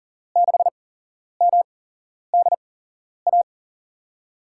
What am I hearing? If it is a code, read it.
Morse code, heard as a keyed tone, about 43 wpm, spaced as for 11 wpm, 700 Hz. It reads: 6MDA
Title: USA Call Signs - 6 Characters - 8x Character Spacing - 40wpm